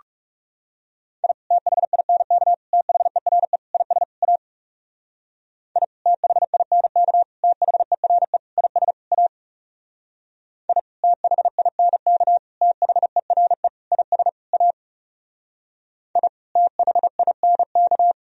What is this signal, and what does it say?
Morse code, heard as a keyed tone, about 37 wpm, 700 Hz. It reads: I THINK THERE IS A I THINK THERE IS A I THINK THERE IS A I THINK
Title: Speed-Racing - Sets of 5 Words 30wpm